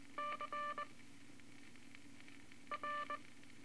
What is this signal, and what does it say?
Morse code, heard as a keyed tone, about 19 wpm, 1180 Hz, with noise. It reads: C R
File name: Morse Code Records / 18 A LCM 164 Ten Words Per Minute